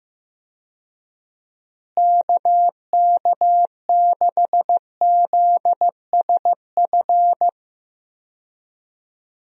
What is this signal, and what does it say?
Morse code, heard as a keyed tone, about 15 wpm, 700 Hz. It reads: KK6ZSF
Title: USA Call Signs - 6 Characters - Repeated 2x - 15wpm